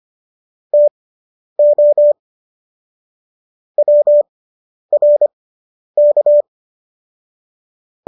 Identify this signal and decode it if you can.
Morse code, heard as a keyed tone, about 25 wpm, spaced as for 10 wpm, 590 Hz. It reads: TO WRK